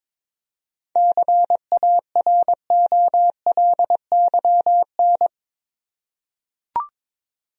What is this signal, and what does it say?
Morse code, heard as a keyed tone, about 22 wpm, 700 Hz. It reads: CAROLYN
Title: USA Common Names - 22wpm